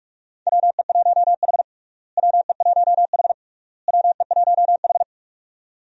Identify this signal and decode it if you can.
Morse code, heard as a keyed tone, about 46 wpm, 700 Hz. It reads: WE1H WE1H WE1H
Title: USA Call Signs - 4 Characters - Repeated 3x - 45wpm